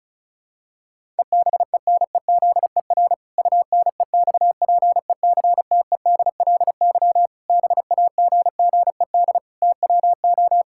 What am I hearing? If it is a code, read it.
Morse code, heard as a keyed tone, about 34 wpm, 700 Hz. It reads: EBENEZER UNEXPECTEDLY BAGGED TWO
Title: Pangram Challenge v2 - 35wpm